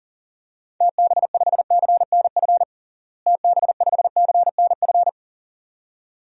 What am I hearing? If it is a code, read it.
Morse code, heard as a keyed tone, about 40 wpm, 700 Hz. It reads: TB5CNF TB5CNF
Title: DXCC-100 Call Signs - 6 Characters - Repeated 2x - 40wpm